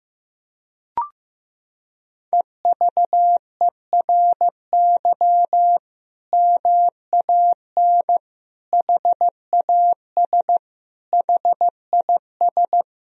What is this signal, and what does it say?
Morse code, heard as a keyed tone, about 16 wpm, 700 Hz. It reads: EVERY MAN HAS HIS
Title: Sentences from Top 1000 Words - Review Part 4 - 15wpm